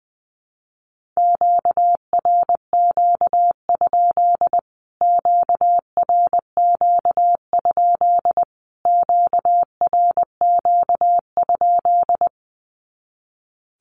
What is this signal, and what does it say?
Morse code, heard as a keyed tone, about 20 wpm, 700 Hz. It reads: QRQ? QRQ? QRQ?
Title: Q-Codes Repeated 3x 20wpm